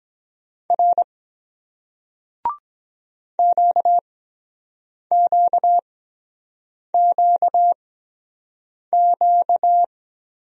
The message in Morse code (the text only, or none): R QQQQ